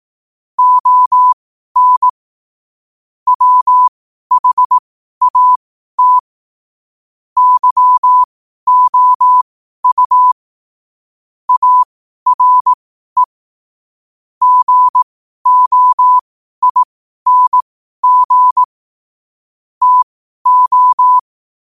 Morse code, heard as a keyed tone, about 18 wpm, spaced as for 12 wpm, 1000 Hz. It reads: ON WHAT YOU ARE GOING TO
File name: qotdfiles / 12wpm